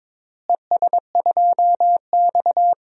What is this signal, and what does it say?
Morse code, heard as a keyed tone, about 22 wpm, 700 Hz. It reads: ES2X
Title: DXCC-100 Call Signs - 4 Characters - Repeated 2x - 22wpm